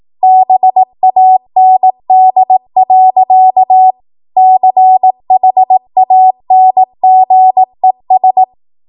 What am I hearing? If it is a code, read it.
Morse code, heard as a keyed tone, about 18 wpm, 750 Hz. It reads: BAND. CHANGES
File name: Morse / 18WPM